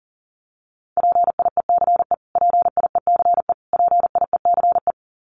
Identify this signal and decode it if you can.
Morse code, heard as a keyed tone, about 40 wpm, 700 Hz. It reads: PIECE PIECE PIECE
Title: Top 500 Words Repeated 40wpm